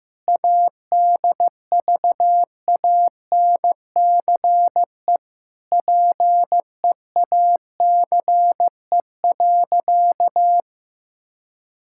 Morse code, heard as a keyed tone, about 15 wpm, 700 Hz. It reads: ADVANCE PEACE.
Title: Sentences from Top 1600 Words 15wpm